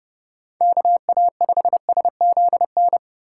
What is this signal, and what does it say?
Morse code, heard as a keyed tone, about 28 wpm, 700 Hz. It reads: KA5SZN